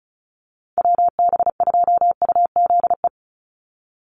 Morse code, heard as a keyed tone, about 34 wpm, 700 Hz. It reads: WB2UZE
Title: Call Signs - 35wpm